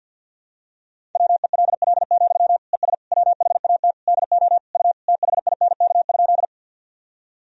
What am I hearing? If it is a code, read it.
Morse code, heard as a keyed tone, about 52 wpm, 700 Hz. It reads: WELL, ES WHAT DO U THINK?
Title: Sentences from Top 200 Words and Abbreviations 50wpm